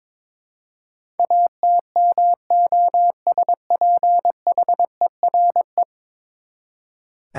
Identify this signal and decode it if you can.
Morse code, heard as a keyed tone, about 22 wpm, 700 Hz. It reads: ATMOSPHERE
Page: morse-code-ninja-practice-22wpm